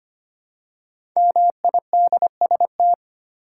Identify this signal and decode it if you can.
Morse code, heard as a keyed tone, about 25 wpm, 700 Hz. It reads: MIDST
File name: 5 Letter Words - First 3 Letters Given - 25wpm